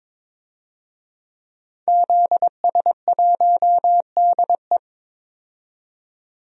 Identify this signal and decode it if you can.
Morse code, heard as a keyed tone, about 22 wpm, 700 Hz. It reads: ZS1DE
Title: DXCC-100 Call Signs - 5 Characters - 22wpm